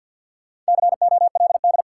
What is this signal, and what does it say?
Morse code, heard as a keyed tone, about 53 wpm, 700 Hz. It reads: COLD
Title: Sets of 5 Words - New Word Drill - 50wpm